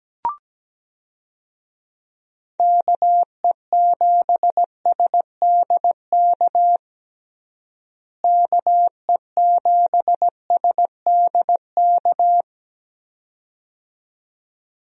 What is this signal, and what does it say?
Morse code, heard as a keyed tone, about 17 wpm, 700 Hz. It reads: KE7SDK KE7SDK
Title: USA Call Signs - 6 Characters - Repeated 2x - 17wpm